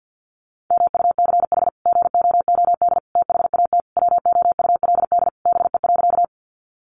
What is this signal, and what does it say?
Morse code, heard as a keyed tone, about 48 wpm, 700 Hz. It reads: MUCH GOOD THAT WOULD BE.
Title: Sentences from Top 200 Words 50wpm